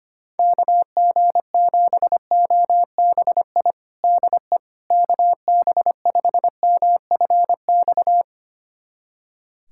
Morse code, heard as a keyed tone, about 25 wpm, 700 Hz. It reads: KG7OBI DE KB5MFX
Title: USA Callsign DE Callsign - 6 Characters - 25wpm